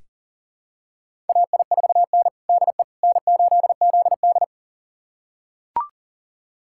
Morse code, heard as a keyed tone, about 40 wpm, 700 Hz. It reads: AI4N DE N8ZD
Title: USA Callsign DE Callsign - 4 Characters - 40wpm